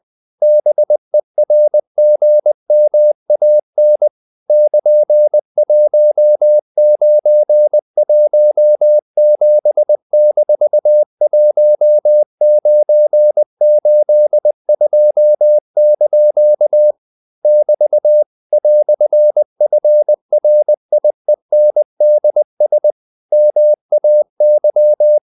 Morse code, heard as a keyed tone, about 20 wpm, 600 Hz. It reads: BERGMAN <KN>1917-1982) <BT> "FRIENDS MAY
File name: CW / Practice / EnglishQuotations0320